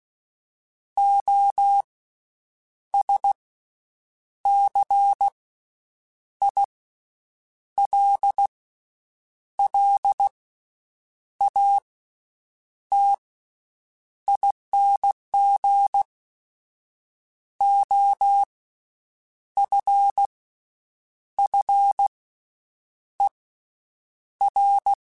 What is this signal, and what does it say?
Morse code, heard as a keyed tone, about 16 wpm, spaced as for 7 wpm, 800 Hz. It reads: OSCILLATING OFFER
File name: Special INGs-mp3s / Special INGs-mp3s / Slow ing @15